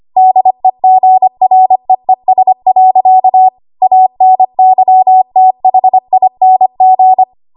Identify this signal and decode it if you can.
Morse code, heard as a keyed tone, about 25 wpm, 750 Hz. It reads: DEGREES. ANYTHING
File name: Archive / 25 WPM / 25WPM